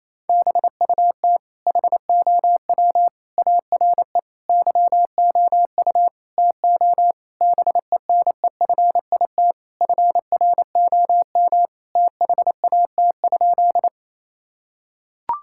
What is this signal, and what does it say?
Morse code, heard as a keyed tone, about 28 wpm, 700 Hz. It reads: BUT HOW ARE YOU TO BENEFIT FROM THAT?
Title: Sentences from Top 1000 Words - Review Part 6 - 28wpm